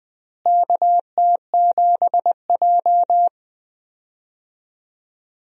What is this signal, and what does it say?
Morse code, heard as a keyed tone, about 20 wpm, 700 Hz. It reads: KT7J
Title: USA Call Signs - 4 Characters - Repeated 2x - 20wpm